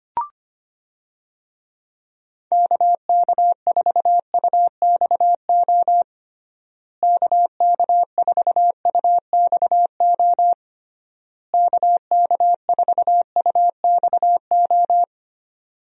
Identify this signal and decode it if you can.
Morse code, heard as a keyed tone, about 25 wpm, 700 Hz. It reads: KK4UXO KK4UXO KK4UXO
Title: USA Call Signs - 6 Characters - Repeated 3x - 25wpm